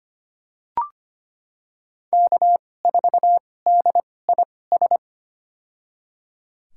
Morse code, heard as a keyed tone, about 25 wpm, spaced as for 18 wpm, 700 Hz. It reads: K4DIS
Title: USA Call Signs - 5 Characters - 2x Character Spacing - 25wpm